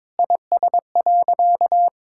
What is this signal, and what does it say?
Morse code, heard as a keyed tone, about 22 wpm, 700 Hz. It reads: IS.
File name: Sentences with 9-Word Phrases - 22wpm